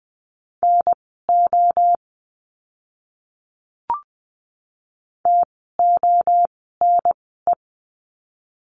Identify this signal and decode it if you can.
Morse code, heard as a keyed tone, about 20 wpm, spaced as for 14 wpm, 700 Hz. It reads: NO TONE
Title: 068 Lesson 12f - Letter d - Words 20wpm